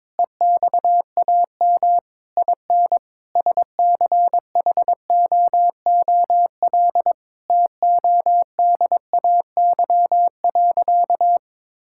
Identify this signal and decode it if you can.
Morse code, heard as a keyed tone, about 22 wpm, 700 Hz. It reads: EXAM IN SCHOOL TODAY.
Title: Sentences from Top 1000 Words - Review Part 5 - 22wpm